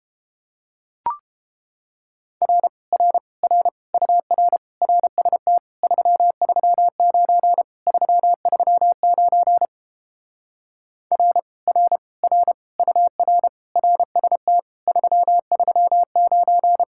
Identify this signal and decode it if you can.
Morse code, heard as a keyed tone, about 29 wpm, 700 Hz. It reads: R R R UR RST 339 339 R R R UR RST 339